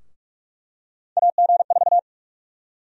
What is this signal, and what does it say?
Morse code, heard as a keyed tone, about 44 wpm, 700 Hz. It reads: AG4
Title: USA Call Sign Prefixes - 45wpm